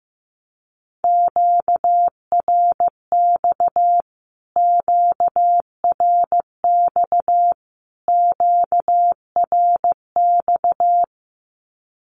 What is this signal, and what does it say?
Morse code, heard as a keyed tone, about 15 wpm, 700 Hz. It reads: QRX QRX QRX
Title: Q-Codes Repeated 3x 15wpm